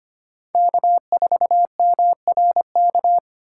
Morse code, heard as a keyed tone, about 25 wpm, 700 Hz. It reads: K4MRK
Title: USA Call Signs - 5 Characters - 4x Character Spacing - 25wpm